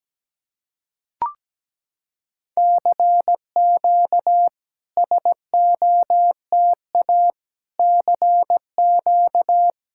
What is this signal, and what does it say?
Morse code, heard as a keyed tone, about 17 wpm, 700 Hz. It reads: CQ SOTA CQ
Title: SOTA QSO Practice 17wpm